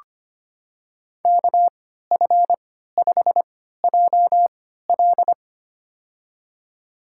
Morse code, heard as a keyed tone, about 25 wpm, spaced as for 14 wpm, 700 Hz. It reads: KF5JL
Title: USA Call Signs - 5 Characters - 3x Character Spacing - 25wpm